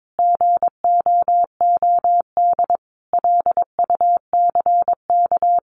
Morse code, heard as a keyed tone, about 22 wpm, 700 Hz. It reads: GOOD LUCK
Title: Common QSO Phrases 22wpm